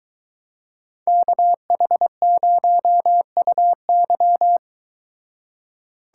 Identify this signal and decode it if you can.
Morse code, heard as a keyed tone, about 22 wpm, 700 Hz. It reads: KH0UY